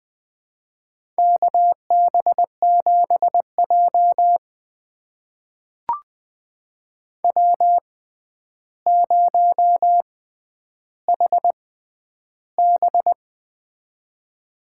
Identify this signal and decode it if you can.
Morse code, heard as a keyed tone, about 20 wpm, spaced as for 10 wpm, 700 Hz. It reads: KB7J W0HB
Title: USA Call Signs - 4 Characters - 6x Character Spacing - 20wpm